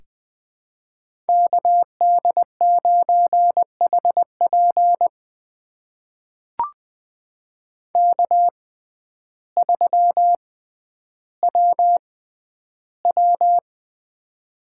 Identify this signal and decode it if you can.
Morse code, heard as a keyed tone, about 21 wpm, spaced as for 11 wpm, 700 Hz. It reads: KD9HP K3WW
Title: USA Call Signs - 6x Character Spacing - 20wpm